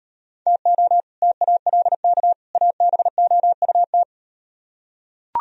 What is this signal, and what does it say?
Morse code, heard as a keyed tone, about 38 wpm, 700 Hz. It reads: TO TALK ABOUT